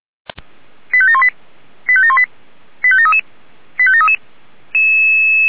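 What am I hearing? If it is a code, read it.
Five-tone selective call (ZVEI), heard as quick runs of five tones, tones of about 70 ms. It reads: 85718 85718 85733 85733
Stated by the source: ZVEI five-tone selective call received on an SDR.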